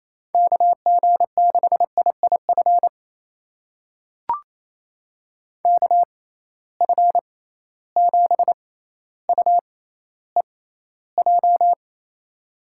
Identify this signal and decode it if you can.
Morse code, heard as a keyed tone, about 28 wpm, spaced as for 14 wpm, 700 Hz. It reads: KG6IIF KF7UEJ
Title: USA Call Signs - 6 Characters - 6x Character Spacing - 28wpm